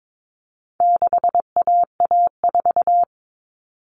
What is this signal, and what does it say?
Morse code, heard as a keyed tone, about 22 wpm, 700 Hz. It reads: 6AA4
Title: All Cut Numbers - 4-Digit 22wpm